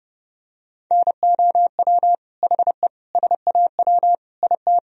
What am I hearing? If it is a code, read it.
Morse code, heard as a keyed tone, about 32 wpm, 700 Hz. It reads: NOW HE SAW IT
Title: Sentences from Top 1000 Words - Review Part 4 - 30wpm